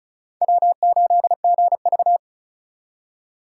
Morse code, heard as a keyed tone, about 44 wpm, 700 Hz. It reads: W8GV